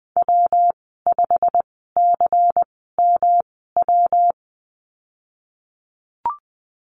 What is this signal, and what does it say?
Morse code, heard as a keyed tone, about 19 wpm, spaced as for 14 wpm, 700 Hz. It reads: W5CMW
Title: 154 Lesson 25g - Letter p - Call Signs 20wpm